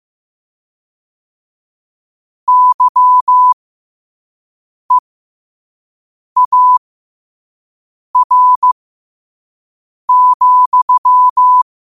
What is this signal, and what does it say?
Morse code, heard as a keyed tone, about 15 wpm, spaced as for 5 wpm, 1000 Hz. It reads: YEAR,
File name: qotdfiles / 5wpm